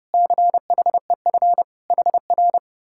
Morse code, heard as a keyed tone, about 32 wpm, 700 Hz. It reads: CHEF HR